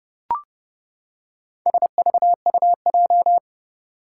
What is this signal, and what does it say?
Morse code, heard as a keyed tone, about 30 wpm, 700 Hz. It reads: SVUJ
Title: Four Challenging Letter-Number Combinations 30wpm